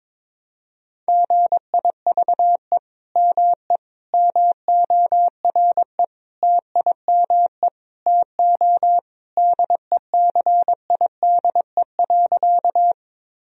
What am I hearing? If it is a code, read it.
Morse code, heard as a keyed tone, about 21 wpm, 700 Hz. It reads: GIVE ME MORE TIME TO DECIDE.
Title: Sentences from Top 1000 Words - Review Part 3 - 22wpm